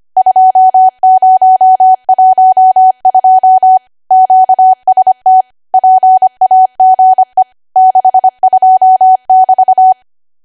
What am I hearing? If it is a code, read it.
Morse code, heard as a keyed tone, about 25 wpm, 750 Hz. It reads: 2012 QST PAGE 62<BT>
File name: cwa / W1AW-Level3 / 25WPM